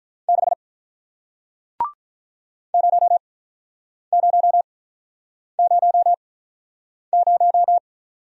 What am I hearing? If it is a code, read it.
Morse code, heard as a keyed tone, about 47 wpm, spaced as for 9 wpm, 700 Hz. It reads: 6 0000